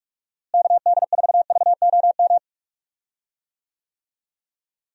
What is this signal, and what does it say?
Morse code, heard as a keyed tone, about 46 wpm, 700 Hz. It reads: KD4VOM
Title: USA Call Signs - 6 Characters - Repeated 3x - 45wpm